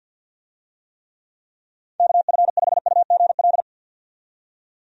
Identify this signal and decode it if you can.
Morse code, heard as a keyed tone, about 53 wpm, 700 Hz. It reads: KF5UGL
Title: USA Call Signs - 6 Characters - 50wpm